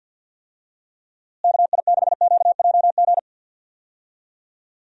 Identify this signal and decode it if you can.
Morse code, heard as a keyed tone, about 51 wpm, 700 Hz. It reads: KI6QJG
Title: USA Call Signs - 6 Characters - Repeated 2x - 50wpm